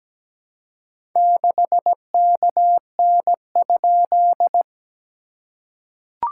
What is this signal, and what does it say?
Morse code, heard as a keyed tone, about 17 wpm, 700 Hz. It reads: 6KN?
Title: SOTA QSO Practice 17wpm